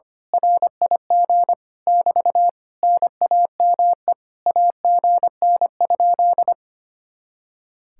Morse code, heard as a keyed tone, about 24 wpm, 700 Hz. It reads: RIG <BT> NAME AGN?
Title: Speed-Racing Common QSO Phrases - 25wpm